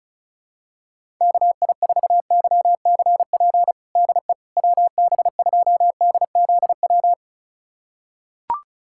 Morse code, heard as a keyed tone, about 36 wpm, 700 Hz. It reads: KI4YCP DE WB2DZW
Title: USA Callsign DE Callsign - 6 Characters - 35wpm